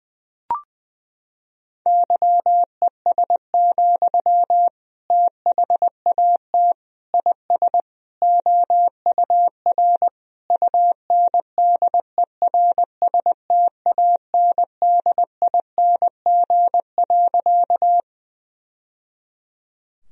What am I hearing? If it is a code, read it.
Morse code, heard as a keyed tone, about 19 wpm, 700 Hz. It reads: YES, THAT IS OUR UNDERSTANDING.
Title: Sentences from Top 1000 Words - Review Part 3 - 20wpm